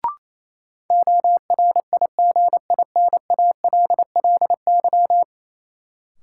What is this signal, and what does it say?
Morse code, heard as a keyed tone, about 28 wpm, 700 Hz. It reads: ORIGINALLY
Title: Sentences from Top 1500 Words - New Word Drill - 28wpm